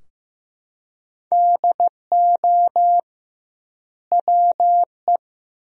Morse code, heard as a keyed tone, about 15 wpm, 700 Hz. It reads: DO WE
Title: 2x Word-Spacing - Sentences from Top 300 Words - 15wpm